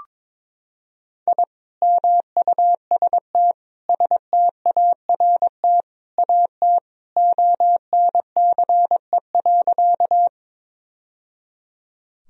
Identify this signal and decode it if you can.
Morse code, heard as a keyed tone, about 22 wpm, 700 Hz. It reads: I MUST START AT ONCE.
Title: Sentences from Top 1000 Words - Review Part 1 - 22wpm